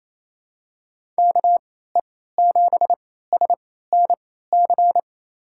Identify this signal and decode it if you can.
Morse code, heard as a keyed tone, about 28 wpm, spaced as for 16 wpm, 700 Hz. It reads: KE7SNC
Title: USA Call Signs - 3x Character Spacing - 28wpm